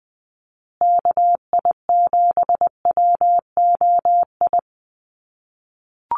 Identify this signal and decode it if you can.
Morse code, heard as a keyed tone, about 20 wpm, 700 Hz. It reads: KI7WOI
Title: USA Call Signs Repeated 2x 20wpm